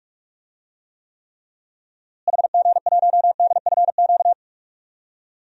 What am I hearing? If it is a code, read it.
Morse code, heard as a keyed tone, about 42 wpm, 700 Hz. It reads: HG1DFQ
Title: DXCC-100 Call Signs - 6 Characters - Repeated 3x - 45wpm